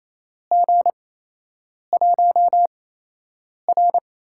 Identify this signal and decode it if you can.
Morse code, heard as a keyed tone, about 28 wpm, spaced as for 8 wpm, 700 Hz. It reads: G1R
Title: USA Call Signs - 8x Character Spacing - 28wpm